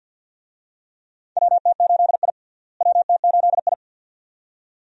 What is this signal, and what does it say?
Morse code, heard as a keyed tone, about 52 wpm, 700 Hz. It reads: WT8I WT8I
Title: USA Call Signs - 4 Characters - Repeated 2x - 50wpm